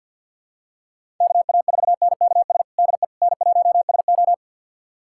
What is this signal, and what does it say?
Morse code, heard as a keyed tone, about 51 wpm, 700 Hz. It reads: KA4NKS DE N1SO
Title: USA Callsign DE Callsign 50wpm